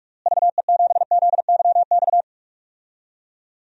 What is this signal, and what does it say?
Morse code, heard as a keyed tone, about 46 wpm, 700 Hz. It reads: VE7ZYX
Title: DXCC-100 Call Signs - 6 Characters - Repeated 2x - 45wpm